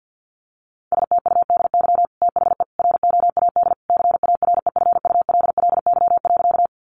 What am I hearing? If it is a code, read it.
Morse code, heard as a keyed tone, about 50 wpm, 700 Hz. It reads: STUDY THE ROAD CAREFULLY.
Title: Sentences from Top 1300 Words 50wpm